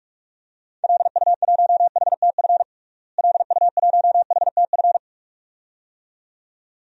Morse code, heard as a keyed tone, about 42 wpm, 700 Hz. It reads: LU1HTF LU1HTF
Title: DXCC-100 Call Signs - 6 Characters - Repeated 2x - 45wpm